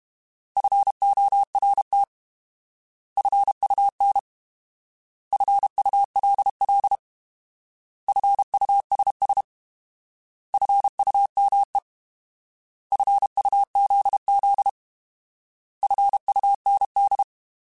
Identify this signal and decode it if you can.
Morse code, heard as a keyed tone, about 32 wpm, 800 Hz. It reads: FORT FUN FULL FUSS FUME FUZZ FUND
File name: Alphabetical / Alphabetical mp3s / Fs @30